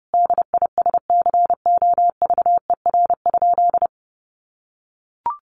Morse code, heard as a keyed tone, about 30 wpm, 700 Hz. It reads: DISCOVER?
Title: Sentences from Top 1100 Words 30wpm